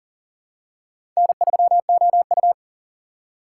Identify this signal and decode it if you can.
Morse code, heard as a keyed tone, about 43 wpm, 690 Hz. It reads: N3OU
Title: USA Call Signs - 4 Characters - 2x Character Spacing - 40wpm